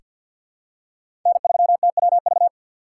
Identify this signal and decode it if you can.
Morse code, heard as a keyed tone, about 48 wpm, 700 Hz. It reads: N3TWV
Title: USA Call Signs - 5 Characters - 4x Character Spacing - 50wpm